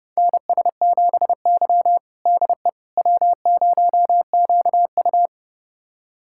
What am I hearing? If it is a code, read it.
Morse code, heard as a keyed tone, about 32 wpm, 700 Hz. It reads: NS7Y DE W0QU
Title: USA Callsign DE Callsign - 4 Characters - 30wpm